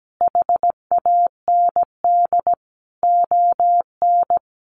Morse code, heard as a keyed tone, about 17 wpm, 700 Hz. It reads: HAND ON
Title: Sentences from Top 400 Words 17wpm